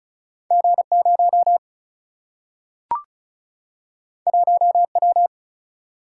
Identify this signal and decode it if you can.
Morse code, heard as a keyed tone, about 36 wpm, 700 Hz. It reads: G0 1W
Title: Single-Letter Single-Number 35wpm